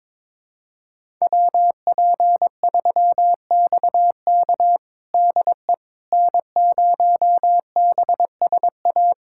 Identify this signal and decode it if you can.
Morse code, heard as a keyed tone, about 22 wpm, 700 Hz. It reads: WP3XK DE N0BSA
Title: USA Callsign DE Callsign - 5 Characters - 22wpm